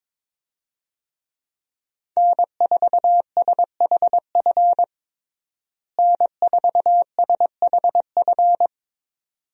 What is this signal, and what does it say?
Morse code, heard as a keyed tone, about 22 wpm, 700 Hz. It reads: N4SHF N4SHF
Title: USA Call Signs - 5 Characters - Repeated 2x - 22wpm